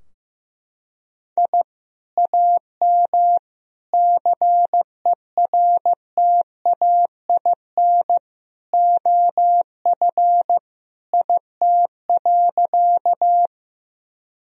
Morse code, heard as a keyed tone, about 15 wpm, 700 Hz. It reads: I AM CERTAIN OF IT.
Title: Sentences from Top 1000 Words - Review Part 2 - 15wpm